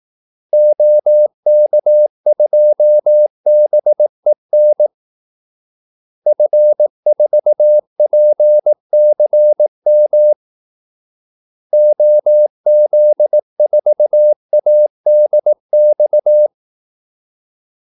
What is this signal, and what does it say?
Morse code, heard as a keyed tone, about 18 wpm, 600 Hz. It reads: OK2BEN F4PCM OZ4ADX
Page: Kaldesignaler 18 wpm